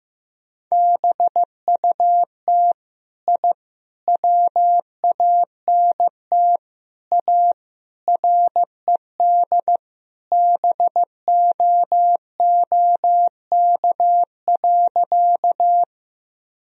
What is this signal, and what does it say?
Morse code, heard as a keyed tone, about 15 wpm, 700 Hz. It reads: BUT I WANT A RED BOOK.
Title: Sentences from Top 1000 Words - Review Part 2 - 15wpm